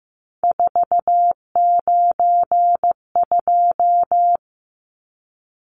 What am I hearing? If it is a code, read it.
Morse code, heard as a keyed tone, about 15 wpm, 700 Hz. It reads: 492
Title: Numbers - 3-Digit 15wpm